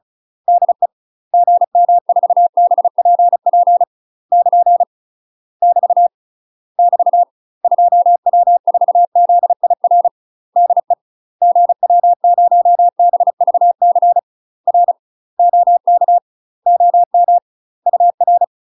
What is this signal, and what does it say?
Morse code, heard as a keyed tone, about 36 wpm, 700 Hz. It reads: DE GM4BPP <KN> <BT> <BT> 2W4ZIR DE GW0BVC R OK OM UR
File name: UK MASTER QSOs TRAINING / 35WPM / UK MASTER QSOs 0032